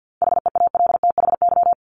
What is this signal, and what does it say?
Morse code, heard as a keyed tone, about 50 wpm, 700 Hz. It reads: HEALTHY
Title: Top 501-750 Words 50wpm